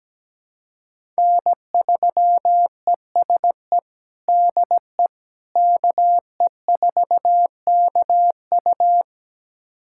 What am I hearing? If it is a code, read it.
Morse code, heard as a keyed tone, about 17 wpm, 700 Hz. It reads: N3ESE DE KE4KU
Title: USA Callsign DE Callsign - 5 Characters - 17wpm